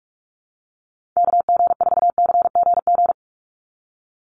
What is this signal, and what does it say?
Morse code, heard as a keyed tone, about 46 wpm, 700 Hz. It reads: KG4CGG
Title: USA Call Signs 45wpm